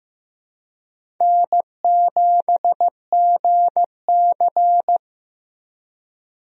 Morse code, heard as a keyed tone, about 15 wpm, 700 Hz. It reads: N7GC
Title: USA Call Signs - 4 Characters - Repeated 2x - 15wpm